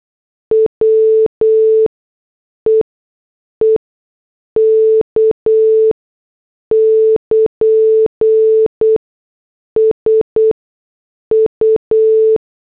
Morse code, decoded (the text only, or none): WEEK <KN> SU